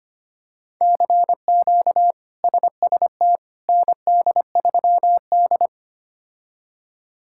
Morse code, heard as a keyed tone, about 25 wpm, 700 Hz. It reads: CQ SST ND3D
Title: K1USN SST QSO Practice - 25wpm